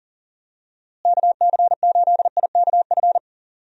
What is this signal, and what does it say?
Morse code, heard as a keyed tone, about 40 wpm, 700 Hz. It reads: KC8IKF
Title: USA Call Signs - 6 Characters - Repeated 3x - 40wpm